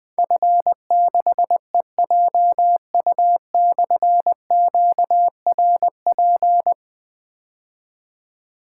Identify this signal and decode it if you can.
Morse code, heard as a keyed tone, about 20 wpm, 700 Hz. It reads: F6EJU/QRP
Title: Difficult Call Signs 20wpm